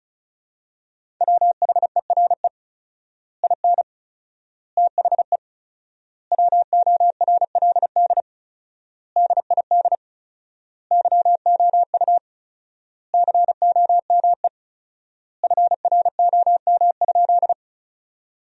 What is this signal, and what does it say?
Morse code, heard as a keyed tone, about 35 wpm, 700 Hz. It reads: WHERE IN THE WORLD DID YOU COME FROM?
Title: 4x Word-Spacing - Sentences from Top 200 Words - 35wpm